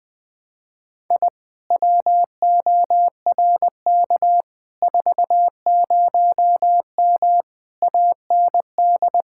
Morse code, heard as a keyed tone, about 20 wpm, 700 Hz. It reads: I WORK 40M AND
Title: Speed-Racing Common QSO Phrases Encore - 20wpm